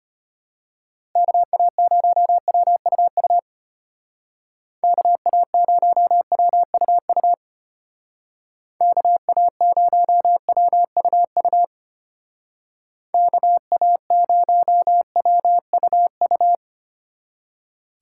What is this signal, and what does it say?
Morse code, heard as a keyed tone, about 32 wpm, 700 Hz. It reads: KA0WUU KA0WUU KA0WUU KA0WUU